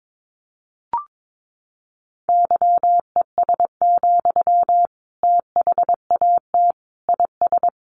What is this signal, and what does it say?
Morse code, heard as a keyed tone, about 22 wpm, 700 Hz. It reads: YES, THAT IS
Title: Sentences from Top 500 Words 22wpm